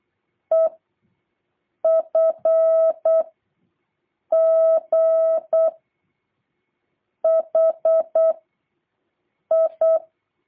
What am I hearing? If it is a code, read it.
Morse code, heard as a keyed tone, about 8 wpm, 650 Hz. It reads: E F G H I